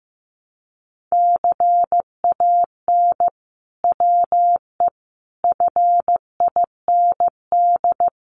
Morse code, heard as a keyed tone, about 15 wpm, 700 Hz. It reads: CAN WE FIND
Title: Sentences from Top 1500 Words 15wpm